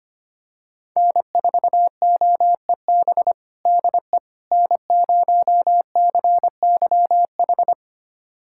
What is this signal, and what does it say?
Morse code, heard as a keyed tone, about 25 wpm, 700 Hz. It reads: N4OEB DE N0CYH
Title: USA Callsign DE Callsign 25wpm